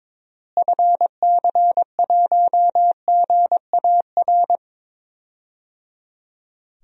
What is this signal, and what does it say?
Morse code, heard as a keyed tone, about 21 wpm, 700 Hz. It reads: FC1GAR
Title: DXCC-100 Call Signs - 6 Characters - Repeated 3x - 22wpm